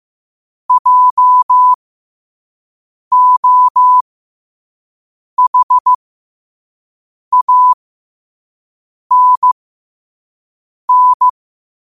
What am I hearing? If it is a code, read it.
Morse code, heard as a keyed tone, about 15 wpm, spaced as for 5 wpm, 1000 Hz. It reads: JOHANN